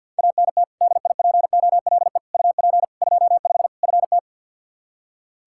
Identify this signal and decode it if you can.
Morse code, heard as a keyed tone, about 52 wpm, 700 Hz. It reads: ANT DIPOLE UP 25 FT
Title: Practice QSO - Exchange 2 - Info - 50wpm